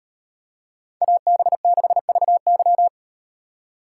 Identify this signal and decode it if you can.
Morse code, heard as a keyed tone, about 38 wpm, 700 Hz. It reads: AB6VY